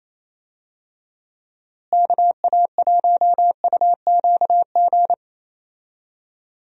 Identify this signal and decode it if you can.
Morse code, heard as a keyed tone, about 27 wpm, 700 Hz. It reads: KA1UQG
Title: USA Call Signs - 6 Characters - Repeated 3x - 28wpm